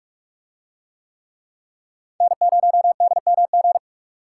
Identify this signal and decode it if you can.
Morse code, heard as a keyed tone, about 40 wpm, 700 Hz. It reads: N0DMG